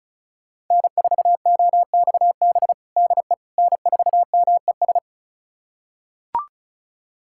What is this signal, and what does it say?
Morse code, heard as a keyed tone, about 36 wpm, 700 Hz. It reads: N4OXB DE N4MES
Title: USA Callsign DE Callsign - 5 Characters - 35wpm